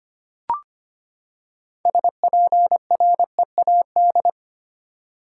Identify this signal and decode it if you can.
Morse code, heard as a keyed tone, about 25 wpm, 700 Hz. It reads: SPREAD
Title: Six Letter Words - 25wpm